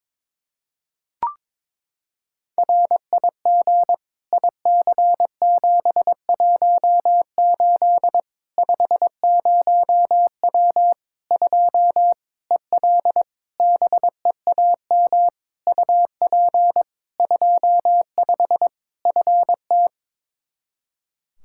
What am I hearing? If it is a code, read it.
Morse code, heard as a keyed tone, about 22 wpm, 700 Hz. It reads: RIG IC718 50W 2 EL BEAM UP 25 FT
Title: K1USN SST QSO Practice 22wpm